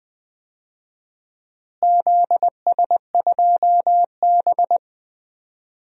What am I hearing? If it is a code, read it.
Morse code, heard as a keyed tone, about 20 wpm, 700 Hz. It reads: ZS2B
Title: DXCC-100 Call Signs - 4 Characters - Repeated 2x - 20wpm